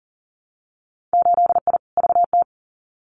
Morse code, heard as a keyed tone, about 42 wpm, 700 Hz. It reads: 8I VT
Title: Winter Field Day Exchanges Encore 40wpm